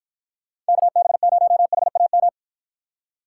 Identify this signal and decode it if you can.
Morse code, heard as a keyed tone, about 50 wpm, 700 Hz. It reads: KB0HAM